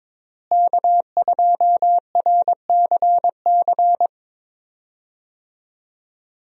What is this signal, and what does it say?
Morse code, heard as a keyed tone, about 22 wpm, 700 Hz. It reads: K2RCC
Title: USA Call Signs - 5 Characters - Repeated 2x - 22wpm